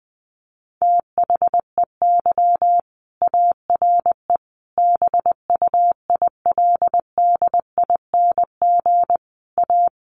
Morse code, heard as a keyed tone, about 20 wpm, 700 Hz. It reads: THEY ARE BUILDING A
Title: Sentences from Top 700 Words 20wpm